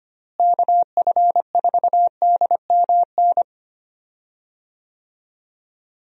Morse code, heard as a keyed tone, about 25 wpm, 700 Hz. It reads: KF4DMN
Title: USA Call Signs - 6 Characters - Repeated 3x - 25wpm